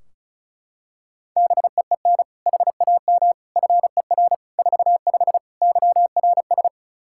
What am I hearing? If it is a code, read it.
Morse code, heard as a keyed tone, about 36 wpm, 700 Hz. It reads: BEEN HAM FER 45 YRS
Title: Practice QSO - Exchange 3 - More Info - 35wpm